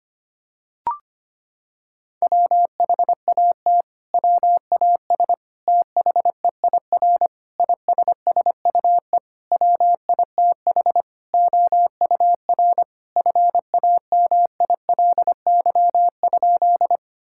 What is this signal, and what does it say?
Morse code, heard as a keyed tone, about 25 wpm, 700 Hz. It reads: WHAT WAS THEIR ISSUE WITH OUR FAMILY?
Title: Sentences from Top 1000 Words - Review Part 6 - 25wpm